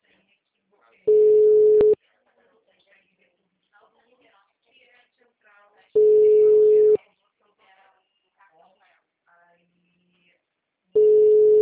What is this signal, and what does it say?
Ringback tone (Europe, 425 Hz), with speech.